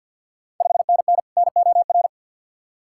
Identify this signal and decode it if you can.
Morse code, heard as a keyed tone, about 52 wpm, 690 Hz. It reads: 5NN NOR
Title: Ontario QSO Party Exchange 50wpm